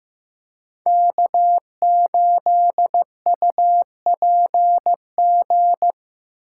Morse code, heard as a keyed tone, about 15 wpm, 700 Hz. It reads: K8UPG
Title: USA Callsign DE Callsign 15wpm